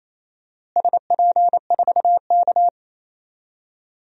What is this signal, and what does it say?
Morse code, heard as a keyed tone, about 28 wpm, 700 Hz. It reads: SP4K
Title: DXCC-100 Call Signs - 4 Characters - Repeated 3x - 28wpm